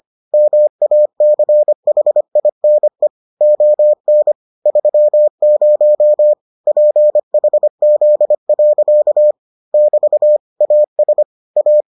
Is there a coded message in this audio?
Yes: Morse code from 0.3 to 11.8 s, heard as a keyed tone, about 25 wpm, 600 Hz. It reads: MACHINE ON 30 PHZ. <BT> AS A